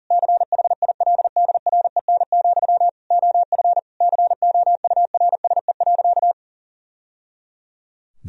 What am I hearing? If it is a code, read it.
Morse code, heard as a keyed tone, about 37 wpm, 700 Hz. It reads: CHILDREN, OF COURSE.
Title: Sentences from Top 1000 Words - Review Part 1 - 40wpm